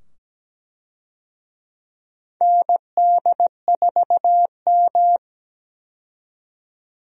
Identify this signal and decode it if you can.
Morse code, heard as a keyed tone, about 17 wpm, 700 Hz. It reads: ND4M
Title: USA Call Signs - 4 Characters - 17wpm